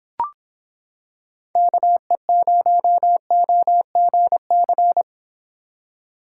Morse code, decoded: KE0OGC

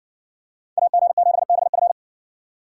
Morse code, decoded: AG7BF

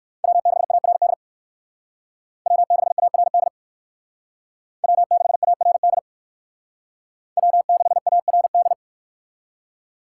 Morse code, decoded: W6ARD W6ARD W6ARD W6ARD

